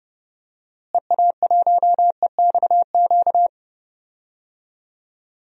Morse code, heard as a keyed tone, about 31 wpm, 700 Hz. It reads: EA1EXQ
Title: DXCC-100 Call Signs - 6 Characters - Repeated 2x - 30wpm